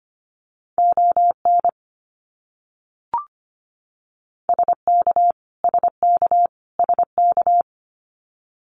Morse code, heard as a keyed tone, about 25 wpm, 700 Hz. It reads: ON SK SK SK
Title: Canadian Provinces and Territories - Repeated 3x - 25wpm